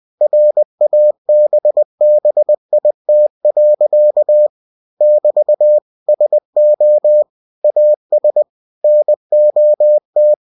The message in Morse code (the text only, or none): RABBIT. <BT> SO AS NOT